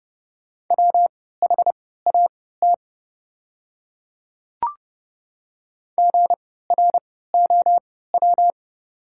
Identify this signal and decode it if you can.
Morse code, heard as a keyed tone, about 30 wpm, spaced as for 17 wpm, 700 Hz. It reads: WHAT GROW
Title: 3x Character Spacing - Top 300 Words - 30wpm